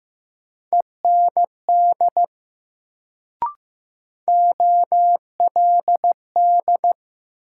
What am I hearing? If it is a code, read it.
Morse code, heard as a keyed tone, about 15 wpm, 700 Hz. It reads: END OLD
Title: Sentences from Top 200 Words - New Word Drill - 15wpm